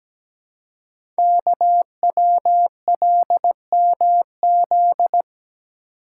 Morse code, heard as a keyed tone, about 17 wpm, 700 Hz. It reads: KWLMZ
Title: Code Groups - 5 Letter:Numbers - 17wpm